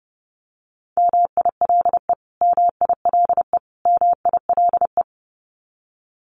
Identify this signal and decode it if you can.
Morse code, heard as a keyed tone, about 30 wpm, 700 Hz. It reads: MILE MILE MILE
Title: Top 500 Words Repeated 30wpm